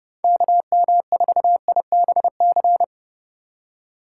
Morse code, heard as a keyed tone, about 32 wpm, 700 Hz. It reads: KM4IBC
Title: USA Call Signs - 6 Characters - 4x Character Spacing - 30wpm